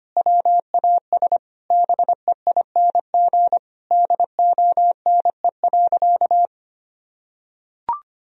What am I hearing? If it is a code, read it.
Morse code, heard as a keyed tone, about 25 wpm, 700 Hz. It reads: WAS BEING DONE.
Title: Sentences from Top 1000 Words - Review Part 3 - 25wpm